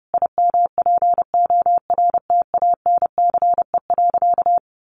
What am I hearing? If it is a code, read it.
Morse code, heard as a keyed tone, about 30 wpm, 700 Hz. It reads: IMPORTANCE.
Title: Sentences from Top 1200 Words 30wpm